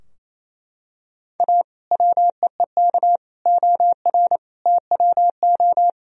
Morse code, heard as a keyed tone, about 28 wpm, 700 Hz. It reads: A WEEK OR TWO
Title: Sentences from Top 1000 Words - Review Part 3 - 28wpm